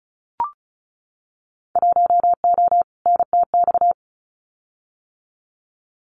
Morse code, heard as a keyed tone, about 34 wpm, 700 Hz. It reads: 1O NTX
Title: Winter Field Day Exchanges 35wpm